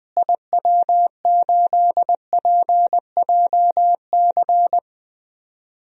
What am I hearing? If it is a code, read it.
Morse code, heard as a keyed tone, about 20 wpm, 700 Hz. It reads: IW8PJC
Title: DXCC-100 Call Signs - 6 Characters - Repeated 2x - 20wpm